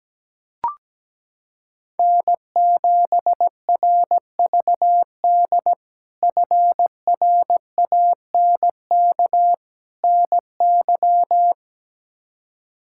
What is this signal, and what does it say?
Morse code, heard as a keyed tone, about 17 wpm, 700 Hz. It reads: N7RVD FRANK NY
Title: K1USN SST QSO Practice - 17wpm